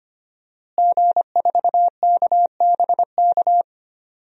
Morse code, heard as a keyed tone, about 25 wpm, 700 Hz. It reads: G4KBK
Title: DXCC-100 Call Signs - 5 Characters - Repeated 2x - 25wpm